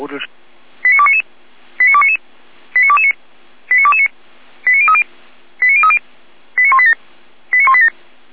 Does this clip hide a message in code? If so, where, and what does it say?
Yes: five-tone selective call (ZVEI) from 0.8 to 7.9 s, heard as quick runs of five tones, tones of about 70 ms, with speech. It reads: 89299 89299 89229 89229 89030 89030 89187 89187